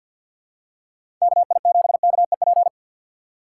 Morse code, heard as a keyed tone, about 47 wpm, 700 Hz. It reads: KI7KEP
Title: USA Call Signs - 6x Character Spacing - 50wpm